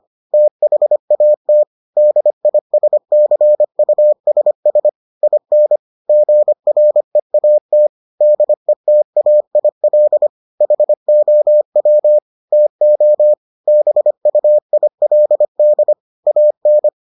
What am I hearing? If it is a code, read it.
Morse code, heard as a keyed tone, about 25 wpm, 600 Hz. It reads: THAT DISCUSS IN GREAT DETAIL HOW TO BUILD AN